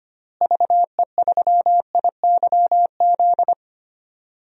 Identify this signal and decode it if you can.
Morse code, heard as a keyed tone, about 25 wpm, 700 Hz. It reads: VE3IYZ
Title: DXCC-100 Call Signs - 6 Characters - Repeated 3x - 25wpm